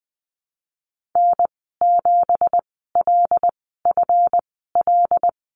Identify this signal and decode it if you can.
Morse code, heard as a keyed tone, about 20 wpm, spaced as for 15 wpm, 700 Hz. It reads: N7LFL
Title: 193 Lesson 31g - Letter v - Call Signs 20wpm